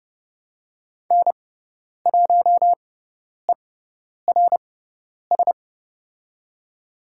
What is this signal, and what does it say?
Morse code, heard as a keyed tone, about 30 wpm, spaced as for 10 wpm, 700 Hz. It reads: N1ERS
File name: Lesson 16g - Letter c - Call Signs 30wpm